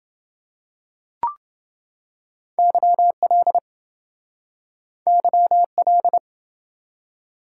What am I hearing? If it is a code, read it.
Morse code, heard as a keyed tone, about 29 wpm, 700 Hz. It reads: YL YL